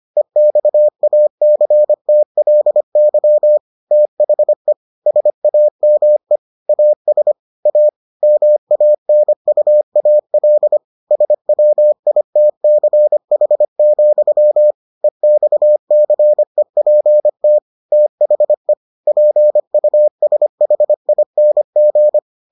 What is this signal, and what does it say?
Morse code, heard as a keyed tone, about 25 wpm, 600 Hz. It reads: EXACTLY THE SAME AS A MANUAL SWITCH, EXCEPT THE PUSHING